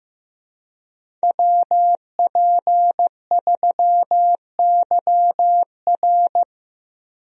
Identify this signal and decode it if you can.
Morse code, heard as a keyed tone, about 15 wpm, 700 Hz. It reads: WP3YR
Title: USA Call Signs - 5 Characters - Repeated 2x - 15wpm